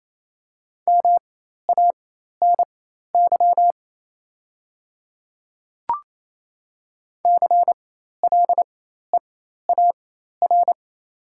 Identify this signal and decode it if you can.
Morse code, heard as a keyed tone, about 28 wpm, spaced as for 13 wpm, 690 Hz. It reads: MANY CLEAR
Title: 4x Character Spacing - Top 500 Words - 28wpm